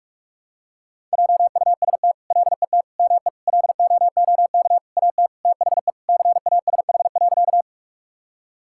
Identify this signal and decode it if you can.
Morse code, heard as a keyed tone, about 46 wpm, 700 Hz. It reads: JUST LET ME LOOK AT THE CASH.
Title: Sentences from Top 1000 Words - Review Part 5 - 45wpm